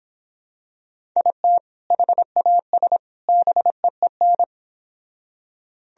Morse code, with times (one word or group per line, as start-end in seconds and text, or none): 1.2-1.6 IT
1.9-3.0 HAS
3.3-4.4 BEEN